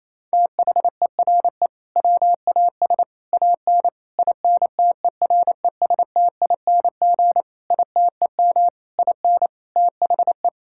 Morse code, heard as a keyed tone, about 27 wpm, 700 Hz. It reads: THERE WAS AN INTERESTING ITEM IN THE
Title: Sentences from Top 1000 Words - Review Part 6 - 28wpm